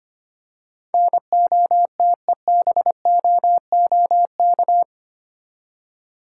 Morse code, heard as a keyed tone, about 25 wpm, 700 Hz. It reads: NOTEBOOK
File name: 8 Letter Words - First 3 Letters Given - 25wpm